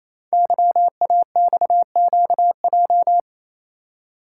Morse code, heard as a keyed tone, about 27 wpm, 700 Hz. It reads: YAXQJ
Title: Code Groups - 5 Letters - 28wpm